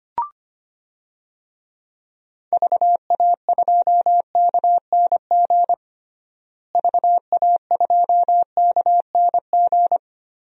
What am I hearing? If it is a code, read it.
Morse code, heard as a keyed tone, about 25 wpm, 700 Hz. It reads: VA2KNG VA2KNG
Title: DXCC-100 Call Signs - 6 Characters - Repeated 2x - 25wpm